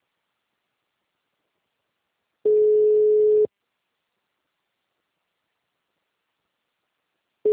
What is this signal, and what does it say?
Ringback tone (Europe, 425 Hz).
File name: out- -2012- - - .302